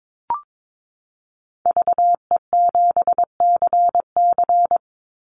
Morse code, heard as a keyed tone, about 22 wpm, 700 Hz. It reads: VE7CC
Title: Call Signs - 22wpm